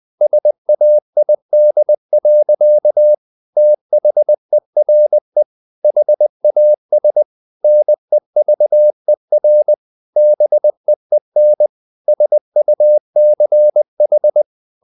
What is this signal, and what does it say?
Morse code, heard as a keyed tone, about 20 wpm, 600 Hz. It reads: SAID. THERE HAS NEVER BEEN SUCH
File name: the old man and the sea part 2 20wpm